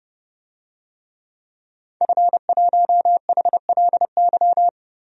Morse code, heard as a keyed tone, about 30 wpm, 700 Hz. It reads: F1HLY
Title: DXCC-100 Call Signs - 5 Characters - Repeated 3x - 30wpm